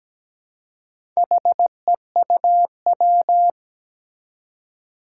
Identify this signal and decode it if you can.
Morse code, heard as a keyed tone, about 17 wpm, 700 Hz. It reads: HEUW
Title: Four Challenging Letter-Number Combinations 17wpm